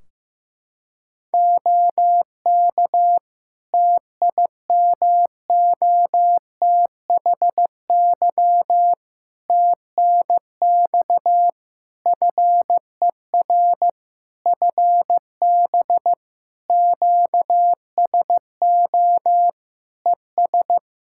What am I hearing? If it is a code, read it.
Morse code, heard as a keyed tone, about 15 wpm, 700 Hz. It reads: OK TIMOTHY TNX FER FB QSO ES
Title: K1USN SST QSO Practice 15wpm